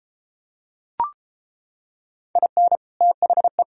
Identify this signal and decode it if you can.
Morse code, heard as a keyed tone, about 33 wpm, 700 Hz. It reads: IN THE